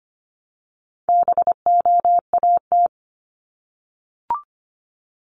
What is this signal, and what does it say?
Morse code, heard as a keyed tone, about 25 wpm, 700 Hz. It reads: BOAT
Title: Multi-letter Phonemes Repeated 3x 25wpm